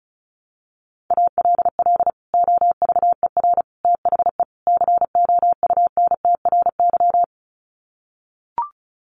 Morse code, heard as a keyed tone, about 34 wpm, 700 Hz. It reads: ALL OVER THE COUNTRY
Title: Sets of 4 Words 35wpm